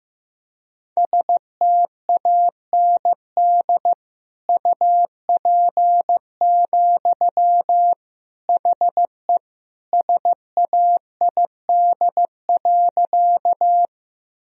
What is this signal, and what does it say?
Morse code, heard as a keyed tone, about 15 wpm, 700 Hz. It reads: STAND UP, HE SAID.
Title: Sentences from Top 1000 Words - Review Part 1 - 15wpm